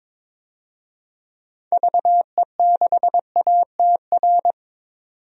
Morse code, heard as a keyed tone, about 22 wpm, 700 Hz. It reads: VE6ATR
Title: DXCC-100 Call Signs - 6 Characters - 22wpm